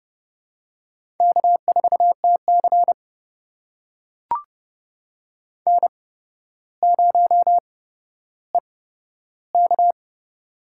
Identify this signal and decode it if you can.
Morse code, heard as a keyed tone, about 28 wpm, spaced as for 13 wpm, 700 Hz. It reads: K4TC N0EK
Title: USA Call Signs - 4 Characters - 8x Character Spacing - 30wpm